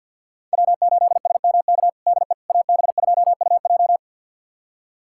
Morse code, heard as a keyed tone, about 52 wpm, 700 Hz. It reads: W8SMK DE AB2UJ
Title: USA Callsign DE Callsign - 5 Characters - 50wpm